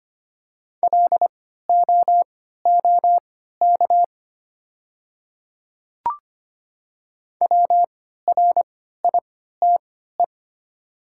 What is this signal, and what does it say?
Morse code, heard as a keyed tone, about 25 wpm, spaced as for 14 wpm, 700 Hz. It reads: LOOK WRITE
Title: 3x Character Spacing - Top 100 Words - 25wpm